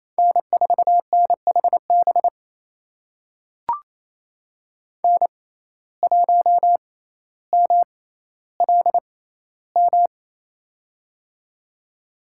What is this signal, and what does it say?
Morse code, heard as a keyed tone, about 27 wpm, spaced as for 14 wpm, 700 Hz. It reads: N4NHB N1MLM
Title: USA Call Signs - 6x Character Spacing - 28wpm